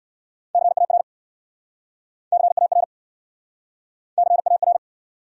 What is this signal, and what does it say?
Morse code, heard as a keyed tone, about 68 wpm, 700 Hz. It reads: CAR CAR CAR